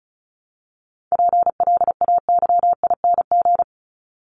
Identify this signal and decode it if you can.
Morse code, heard as a keyed tone, about 34 wpm, 700 Hz. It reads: PLAYING
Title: ING Suffix - Top 100 Words with Base Words 35wpm